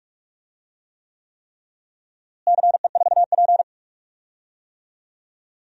Morse code, heard as a keyed tone, about 46 wpm, 700 Hz. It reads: CE4P
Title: DXCC-100 Call Signs - 4 Characters - Repeated 3x - 45wpm